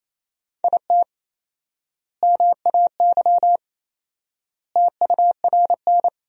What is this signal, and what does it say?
Morse code, heard as a keyed tone, about 28 wpm, 700 Hz. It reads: IT MAY TURN
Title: 4x Word-Spacing - Sentences from Top 400 Words - 28wpm